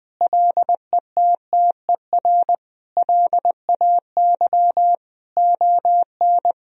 Morse code, heard as a keyed tone, about 20 wpm, 700 Hz. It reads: LETTER LAY ON
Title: Sentences from Top 1000 Words - Review Part 2 - 20wpm